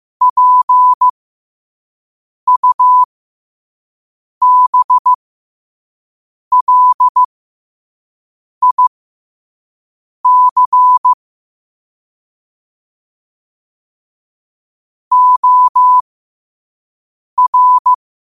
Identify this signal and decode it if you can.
Morse code, heard as a keyed tone, about 15 wpm, spaced as for 5 wpm, 1000 Hz. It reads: PUBLIC OR